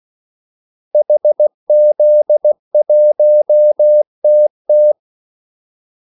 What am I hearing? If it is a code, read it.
Morse code, heard as a keyed tone, about 16 wpm, 600 Hz. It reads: HZ1TT
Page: Kaldesignaler 16 wpm